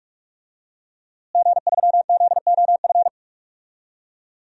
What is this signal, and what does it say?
Morse code, heard as a keyed tone, about 42 wpm, 700 Hz. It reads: G3ZOF
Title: DXCC-100 Call Signs - 5 Characters - Repeated 2x - 45wpm